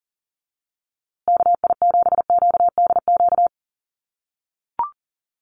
Morse code, heard as a keyed tone, about 40 wpm, 700 Hz. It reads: KI7QDQ
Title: USA Call Signs Repeated 2x 40wpm